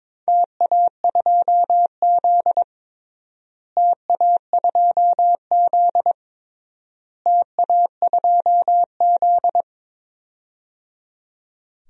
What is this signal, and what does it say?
Morse code, heard as a keyed tone, about 22 wpm, 700 Hz. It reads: TA2Z TA2Z TA2Z
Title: DXCC-100 Call Signs - 4 Characters - Repeated 3x - 22wpm